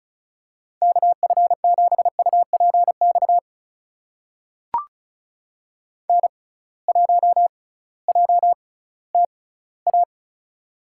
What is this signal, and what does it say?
Morse code, heard as a keyed tone, about 34 wpm, spaced as for 19 wpm, 700 Hz. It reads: KF7UPX N1JTA
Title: USA Call Signs - 6x Character Spacing - 35wpm